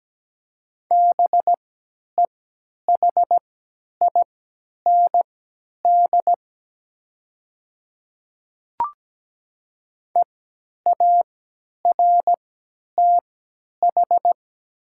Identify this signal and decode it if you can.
Morse code, heard as a keyed tone, about 17 wpm, spaced as for 10 wpm, 700 Hz. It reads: BEHIND EARTH
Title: 3x Character Spacing - Top 500 Words - 17wpm